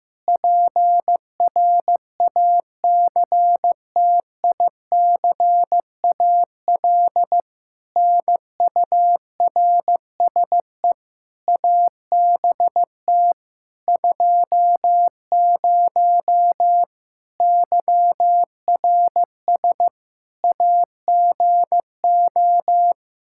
Morse code, heard as a keyed tone, about 15 wpm, 700 Hz. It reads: PRACTICAL NURSE ABT 20 YRS AGO
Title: Practice QSO - Exchange 3 - More Info - 15wpm